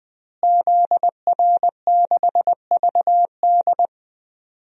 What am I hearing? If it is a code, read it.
Morse code, heard as a keyed tone, about 20 wpm, 700 Hz. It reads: ZR6VD
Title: DXCC-100 Call Signs - 5 Characters - Repeated 2x - 20wpm